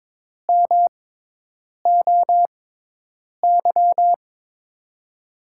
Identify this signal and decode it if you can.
Morse code, heard as a keyed tone, about 22 wpm, spaced as for 8 wpm, 700 Hz. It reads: MOY
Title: USA Call Signs - 6 Characters - 6x Character Spacing - 22wpm